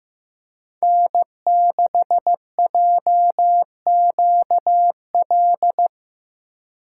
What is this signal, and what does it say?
Morse code, heard as a keyed tone, about 17 wpm, 700 Hz. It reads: N6JQL
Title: Speed Racing - USA 1x3 Call Signs 15wpm